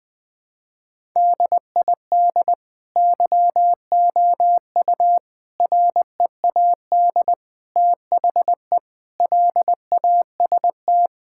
Morse code, heard as a keyed tone, about 19 wpm, 700 Hz. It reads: DID YOU READ THE LAST
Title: Sentences from Top 1000 Words - Review Part 5 - 20wpm